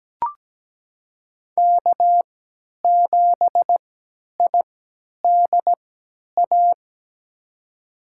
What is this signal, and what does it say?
Morse code, heard as a keyed tone, about 17 wpm, spaced as for 10 wpm, 700 Hz. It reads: K7IDA
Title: USA Call Signs - 5 Characters - 3x Character Spacing - 17wpm